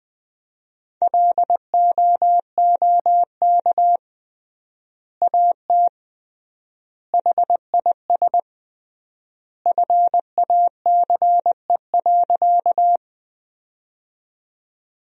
Morse code, heard as a keyed tone, about 20 wpm, 700 Hz. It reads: LOOK AT HIS FACE.
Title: 3x Word-Spacing - Sentences from Top 300 Words - 20wpm